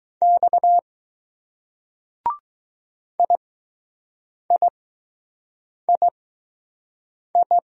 Morse code, heard as a keyed tone, about 20 wpm, spaced as for 6 wpm, 700 Hz. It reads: X IIII